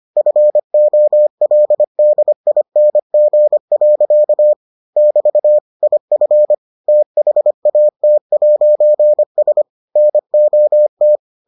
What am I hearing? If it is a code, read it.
Morse code, heard as a keyed tone, about 25 wpm, 600 Hz. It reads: FOLDING. <BT> IF THAT'S NOT